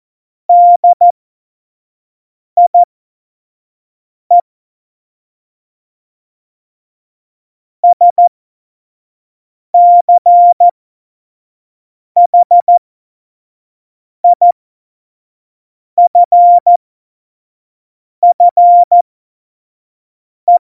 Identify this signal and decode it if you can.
Morse code, heard as a keyed tone, about 14 wpm, spaced as for 5 wpm, 700 Hz. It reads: DIE SCHIFFE
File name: Morsekurs / Witze / Witz 08. August 17, Tempo 25 F70